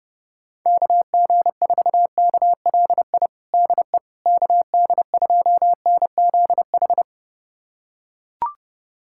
Morse code, heard as a keyed tone, about 32 wpm, 700 Hz. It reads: KG4KLI DE KD2NZH
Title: USA Callsign DE Callsign 30wpm